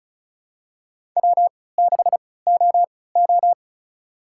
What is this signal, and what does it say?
Morse code, heard as a keyed tone, about 36 wpm, spaced as for 20 wpm, 690 Hz. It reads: W6OO
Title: USA Call Signs - 3x Character Spacing - 35wpm